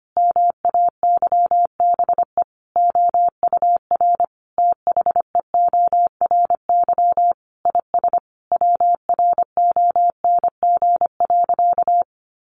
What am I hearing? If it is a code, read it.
Morse code, heard as a keyed tone, about 25 wpm, 700 Hz. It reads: MAYBE OUR THEORY IS WRONG.
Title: A Sound Of Thunder 25wpm